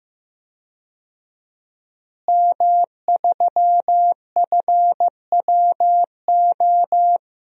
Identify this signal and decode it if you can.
Morse code, heard as a keyed tone, about 16 wpm, 700 Hz. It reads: M3FWO
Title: DXCC-100 Call Signs - 5 Characters - Repeated 3x - 15wpm